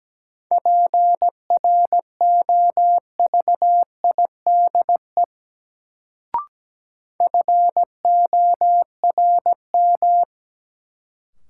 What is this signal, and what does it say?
Morse code, heard as a keyed tone, about 17 wpm, 700 Hz. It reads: PROVIDE FORM
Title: Sentences from Top 1100 Words - New Word Drill Alt - 17wpm